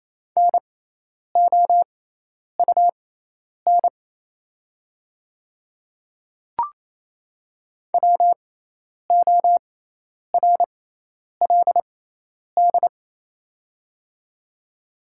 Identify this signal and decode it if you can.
Morse code, heard as a keyed tone, about 28 wpm, spaced as for 10 wpm, 700 Hz. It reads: NOUN WORLD
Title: 6x Character Spacing - Top 500 Words - 28wpm